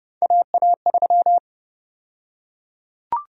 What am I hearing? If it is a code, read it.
Morse code, heard as a keyed tone, about 30 wpm, 700 Hz. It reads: AA3
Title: USA Call Sign Prefixes - 30wpm